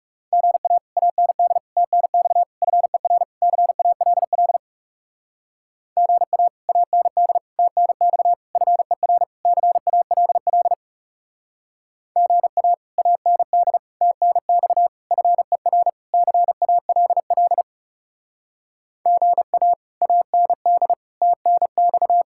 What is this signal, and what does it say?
Morse code, heard as a keyed tone, about 36 wpm, 700 Hz. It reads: GA AND TNX FER CALL GA AND TNX FER CALL GA AND TNX FER CALL GA AND TNX
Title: Speed-Racing Common QSO Phrases - 30wpm